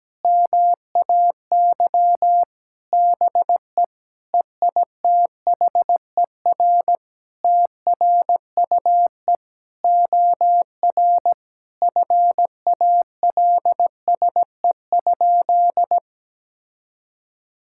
Morse code, heard as a keyed tone, about 17 wpm, 700 Hz. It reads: MAY BE EITHER TRUE OR FALSE?
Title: Sentences from Top 1600 Words 17wpm